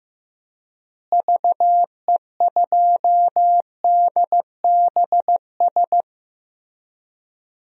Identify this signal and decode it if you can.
Morse code, heard as a keyed tone, about 15 wpm, 700 Hz. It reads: VE2DBS
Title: DXCC-100 Call Signs - 6 Characters - Repeated 2x - 15wpm